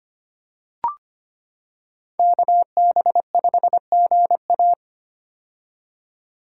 Morse code, heard as a keyed tone, about 25 wpm, 700 Hz. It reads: KB5GA
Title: K1USN SST QSO Practice - 25wpm